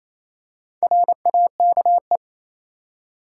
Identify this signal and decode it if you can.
Morse code, heard as a keyed tone, about 27 wpm, 700 Hz. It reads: RAKE
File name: Easy Words - Speed-Racing with Hints - 22wpm